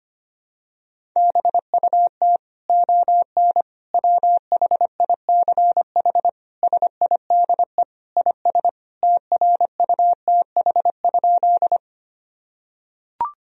Morse code, heard as a keyed tone, about 25 wpm, 700 Hz. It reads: BUT ON WHICH SIDE IS TRUTH?
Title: Sentences from Top 1000 Words - Review Part 1 - 25wpm